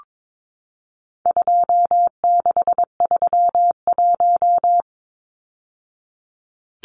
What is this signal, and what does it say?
Morse code, heard as a keyed tone, about 22 wpm, 700 Hz. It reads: 2631
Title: Numbers - 4-Digit 22wpm